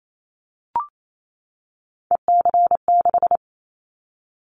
Morse code, heard as a keyed tone, about 28 wpm, 700 Hz. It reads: EC6
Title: Callsign Prefixes for Top 50 DXCC - 28wpm